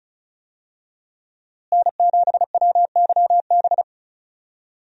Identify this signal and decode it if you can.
Morse code, heard as a keyed tone, about 36 wpm, 700 Hz. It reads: N7WYB
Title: USA Call Signs - 5 Characters - Repeated 3x - 35wpm